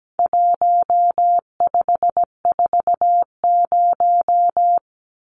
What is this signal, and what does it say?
Morse code, heard as a keyed tone, about 17 wpm, 700 Hz. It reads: 1540
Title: Numbers - 4-Digit 17wpm